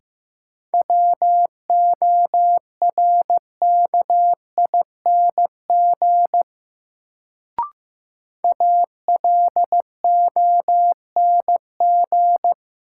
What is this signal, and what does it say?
Morse code, heard as a keyed tone, about 15 wpm, 700 Hz. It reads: WORKING ALONG
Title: Sentences from Top 700 Words - New Word Drill - 15wpm